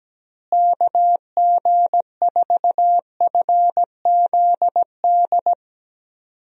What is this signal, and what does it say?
Morse code, heard as a keyed tone, about 17 wpm, 700 Hz. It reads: KG4FZD
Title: USA Call Signs - 6 Characters - Repeated 2x - 17wpm